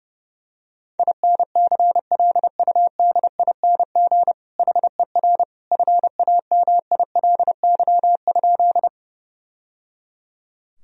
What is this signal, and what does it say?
Morse code, heard as a keyed tone, about 32 wpm, 700 Hz. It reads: INCLUDING HER FAMILY?
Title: Sentences from Top 1000 Words - Review Part 4 - 30wpm